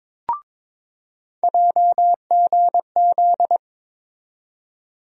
Morse code, heard as a keyed tone, about 22 wpm, 700 Hz. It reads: JGZ
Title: USA Call Sign Postfixes - 3 Letters 22wpm